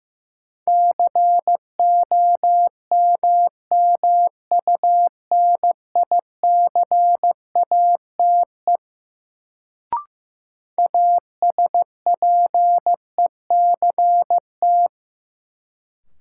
Morse code, heard as a keyed tone, about 15 wpm, 700 Hz. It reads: COMMUNICATE ASPECT
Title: Sentences from Top 1300 Words - New Word Drill Alt - 15wpm